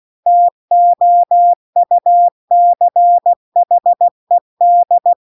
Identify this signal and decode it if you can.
Morse code, heard as a keyed tone, about 16 wpm, 700 Hz. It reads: TOUCHED